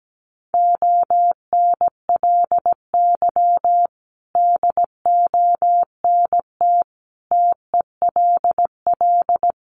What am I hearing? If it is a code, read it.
Morse code, heard as a keyed tone, about 17 wpm, 700 Hz. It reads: ONLY DONT TELL
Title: Sentences from Top 900 Words 17wpm